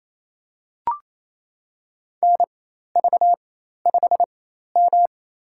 Morse code, heard as a keyed tone, about 29 wpm, spaced as for 13 wpm, 700 Hz. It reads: NV5M
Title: USA Call Signs - 4 Characters - 4x Character Spacing - 28wpm